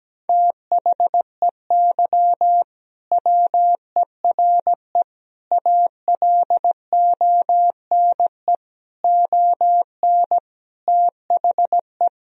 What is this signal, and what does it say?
Morse code, heard as a keyed tone, about 17 wpm, 700 Hz. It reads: THEY WERE ALONE ON THE
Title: Sentences from Top 1000 Words - Review Part 3 - 17wpm